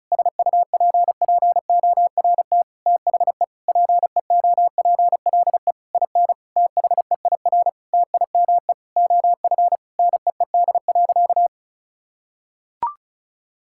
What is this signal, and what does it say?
Morse code, heard as a keyed tone, about 36 wpm, 700 Hz. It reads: SUPPORT THE PEOPLE IN THEIR TIME OF NEED.
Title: Sentences from Top 1000 Words - Review Part 5 - 35wpm